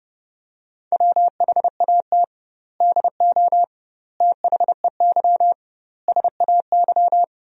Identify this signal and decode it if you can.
Morse code, heard as a keyed tone, about 30 wpm, 700 Hz. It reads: WHAT DO THEY SAY
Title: 2x Word-Spacing - Sentences from Top 200 Words - 30wpm